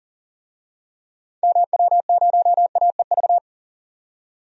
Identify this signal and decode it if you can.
Morse code, heard as a keyed tone, about 37 wpm, 700 Hz. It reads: MW0AEV
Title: DXCC-100 Call Signs - 6 Characters - 40wpm